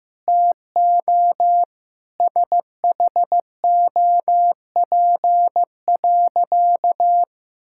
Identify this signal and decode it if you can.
Morse code, heard as a keyed tone, about 15 wpm, 700 Hz. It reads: TO SHOP.
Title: Sentences from Top 1600 Words 15wpm